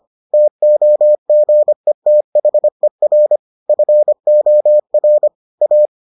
Morse code, heard as a keyed tone, about 25 wpm, 600 Hz. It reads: TOGETHER FOR A